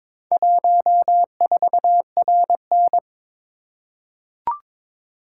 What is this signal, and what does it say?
Morse code, heard as a keyed tone, about 22 wpm, 700 Hz. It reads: 14RN
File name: practice / 22wpm / Four Letter-Number Combinations - 22wpm